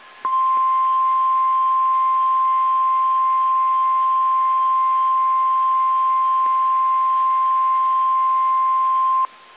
Emergency Alert System (SAME) tones (the 1050 Hz attention tone). It is heard from 0.2 to 9.3 s.
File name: eas / KSDZKDJL / KSDZKDJL